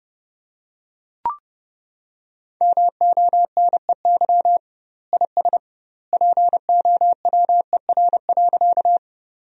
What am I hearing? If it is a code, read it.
Morse code, heard as a keyed tone, about 32 wpm, 700 Hz. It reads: MONEY IS POWER.
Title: 2x Word-Spacing - Sentences from Top 500 Words - 30wpm